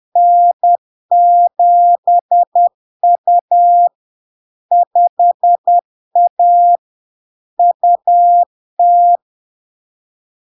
Morse code, heard as a keyed tone, about 10 wpm, 700 Hz. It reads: N7U 5A UT
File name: LARK-FD-2022-Log-in-CW-10wpm-700-Hz